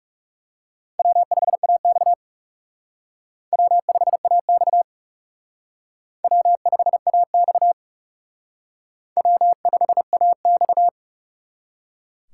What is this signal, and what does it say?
Morse code, heard as a keyed tone, about 37 wpm, 700 Hz. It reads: W5AX W5AX W5AX W5AX
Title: Speed Racing - USA 1x2 Call Signs 30wpm